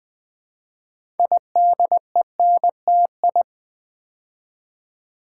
Morse code, heard as a keyed tone, about 20 wpm, 700 Hz. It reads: IDENTI
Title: Build A Word - 3 Letters At A Time - 8-Letter Words - 20wpm